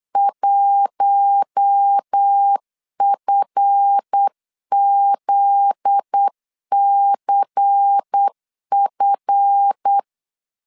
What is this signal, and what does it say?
Morse code, heard as a keyed tone, about 8 wpm, 800 Hz. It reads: 1FZCF
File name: LEZIONI / 10 SIGNS / 10C DOT 8wpm 3649